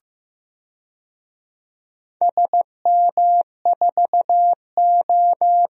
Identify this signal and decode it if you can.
Morse code, heard as a keyed tone, about 15 wpm, 700 Hz. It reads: SM4O
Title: DXCC-100 Call Signs - 4 Characters - Repeated 3x - 15wpm